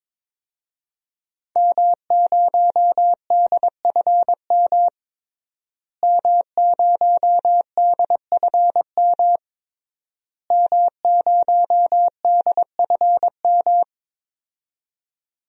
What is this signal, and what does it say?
Morse code, heard as a keyed tone, about 22 wpm, 700 Hz. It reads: M0DFM M0DFM M0DFM
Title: DXCC-100 Call Signs - 5 Characters - Repeated 3x - 22wpm